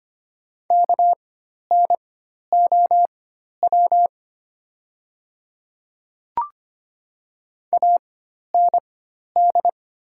Morse code, heard as a keyed tone, about 25 wpm, spaced as for 12 wpm, 700 Hz. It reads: KNOW AND
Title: 4x Character Spacing - Top 200 Words - 25wpm